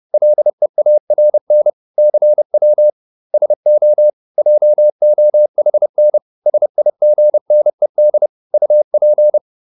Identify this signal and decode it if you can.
Morse code, heard as a keyed tone, about 30 wpm, 600 Hz. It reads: LEARN CW SO JOHN SIGNED UP
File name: SS301 30